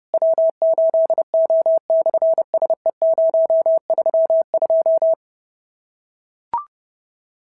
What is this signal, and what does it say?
Morse code, heard as a keyed tone, about 30 wpm, 690 Hz. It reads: W8O/SE032
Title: SOTA QSO Practice 30wpm